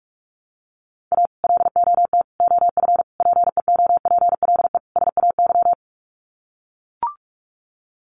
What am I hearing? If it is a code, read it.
Morse code, heard as a keyed tone, about 44 wpm, 700 Hz. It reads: A LOT OF PEOPLE SAY
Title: Sets of 5 Words Encore2 45wpm